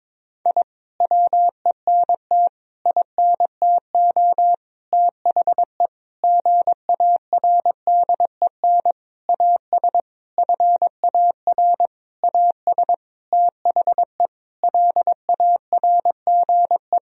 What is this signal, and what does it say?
Morse code, heard as a keyed tone, about 21 wpm, 700 Hz. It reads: I WENT INTO THE GARDEN AS FAR AS THE LARGE
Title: Sentences from Top 1000 Words - Review Part 4 - 22wpm